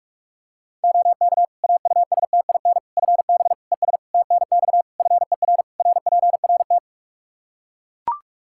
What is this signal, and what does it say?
Morse code, heard as a keyed tone, about 41 wpm, 700 Hz. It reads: OK AUSTIN FB ES TNX FER RPRT
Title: Practice QSO - Exchange 2 - Info - 45wpm